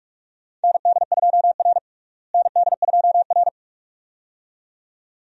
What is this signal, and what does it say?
Morse code, heard as a keyed tone, about 42 wpm, 700 Hz. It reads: ND2R ND2R
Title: USA Call Signs - 4 Characters - Repeated 2x - 45wpm